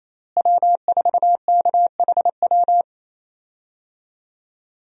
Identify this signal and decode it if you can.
Morse code, heard as a keyed tone, about 27 wpm, 700 Hz. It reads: W4KHW
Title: USA Call Signs - 5 Characters - Repeated 2x - 28wpm